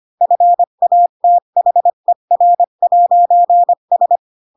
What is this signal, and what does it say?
Morse code, heard as a keyed tone, about 25 wpm, 700 Hz. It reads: FATHER'S
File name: Pinocchio / Pinocchio ch8 20 25 ik1hge